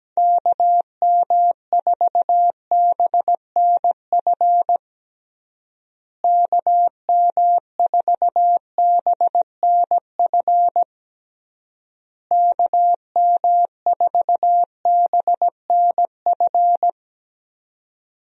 Morse code, heard as a keyed tone, about 17 wpm, 700 Hz. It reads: KM4BNF KM4BNF KM4BNF
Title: USA Call Signs - 6 Characters - Repeated 3x - 17wpm